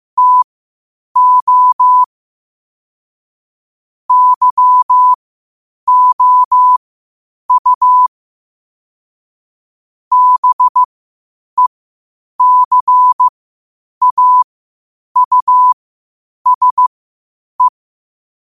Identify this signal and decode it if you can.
Morse code, heard as a keyed tone, about 15 wpm, spaced as for 8 wpm, 1000 Hz. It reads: TO YOU BECAUSE
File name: qotdfiles / 8wpm